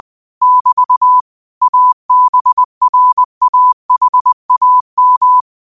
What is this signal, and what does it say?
Morse code, heard as a keyed tone, about 20 wpm, 1000 Hz. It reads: <BT> ABRAHAM
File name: qotdfiles / 20wpm